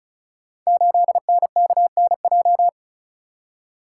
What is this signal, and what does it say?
Morse code, heard as a keyed tone, about 36 wpm, 700 Hz. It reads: 8NKNJ
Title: Code Groups - 5 Letter:Numbers - 35wpm